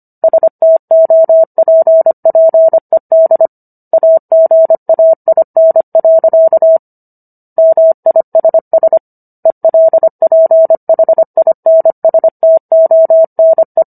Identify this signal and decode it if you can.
Morse code, heard as a keyed tone, about 25 wpm, 650 Hz. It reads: STOPPED AGAIN. MISS ELPHINSTONE